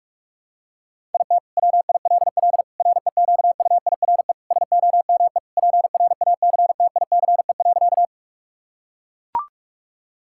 Morse code, heard as a keyed tone, about 44 wpm, 700 Hz. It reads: IT WILL REQUIRE SOME PRACTICE.
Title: Sentences from Top 1000 Words - Review Part 6 - 45wpm